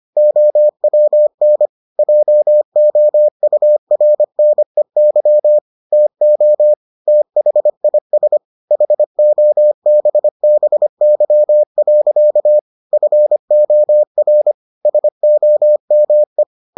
Morse code, heard as a keyed tone, about 25 wpm, 600 Hz. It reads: OWN JOURNEY TO THIS HOBBY. FOR SOME